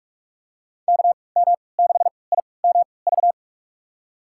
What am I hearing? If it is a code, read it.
Morse code, heard as a keyed tone, about 45 wpm, spaced as for 25 wpm, 700 Hz. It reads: KM6IMV
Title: USA Call Signs - 6 Characters - 3x Character Spacing - 45wpm